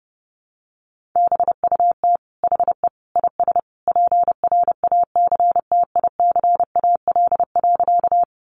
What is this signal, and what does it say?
Morse code, heard as a keyed tone, about 30 wpm, 700 Hz. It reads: BUT HE IS PRACTICAL.
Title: Sentences from Top 1400 Words 30wpm